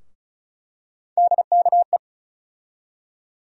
Morse code, heard as a keyed tone, about 34 wpm, 700 Hz. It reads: DKE
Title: USA Call Sign Postfixes - 3 Letters 35wpm